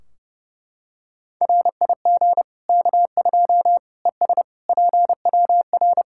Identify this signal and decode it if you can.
Morse code, heard as a keyed tone, about 30 wpm, 700 Hz. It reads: RIG K2 ES PWR
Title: Practice QSO - Exchange 2 - Info - 30wpm